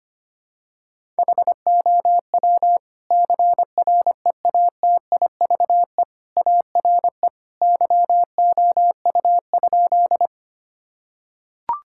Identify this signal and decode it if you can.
Morse code, heard as a keyed tone, about 25 wpm, 700 Hz. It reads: HOW CREATIVE ARE YOU?
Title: Sentences from Top 1000 Words - Review Part 5 - 25wpm